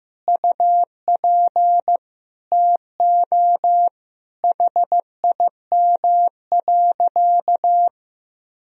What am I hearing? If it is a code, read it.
Morse code, heard as a keyed tone, about 16 wpm, 700 Hz. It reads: UP TO HIM.
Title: Sentences from Top 1000 Words - Review Part 2 - 15wpm